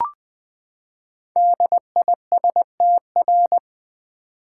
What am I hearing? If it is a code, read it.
Morse code, heard as a keyed tone, about 20 wpm, 700 Hz. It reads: DISTR
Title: Build A Word - 5 Letters At A Time - 12-Letter Words - 20wpm